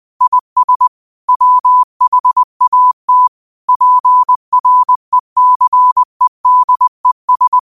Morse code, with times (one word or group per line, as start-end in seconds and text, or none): 0.2-0.9 IS
1.3-3.3 WHAT
3.7-7.6 PRECEDES